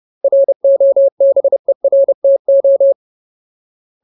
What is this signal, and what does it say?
Morse code, heard as a keyed tone, about 30 wpm, 550 Hz. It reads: ROBERTO
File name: Ultimate Word List / 30WPM Ultimate